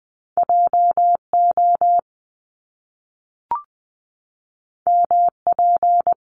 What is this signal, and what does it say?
Morse code, heard as a keyed tone, about 20 wpm, 700 Hz. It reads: JO MP
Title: Top Two Letter Combinations in Words 20wpm